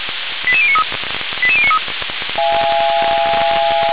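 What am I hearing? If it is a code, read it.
Five-tone selective call (ZVEI), heard as quick runs of five tones, tones of about 70 ms. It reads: 88093 88093
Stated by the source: ZVEI five-tone call received by SDR, from the audio server of the Feuerwehr Orlamünde (fire brigade).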